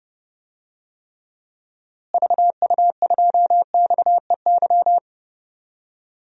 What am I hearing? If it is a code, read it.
Morse code, heard as a keyed tone, about 30 wpm, 700 Hz. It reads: VU2XEY
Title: DXCC-100 Call Signs - 6 Characters - Repeated 2x - 30wpm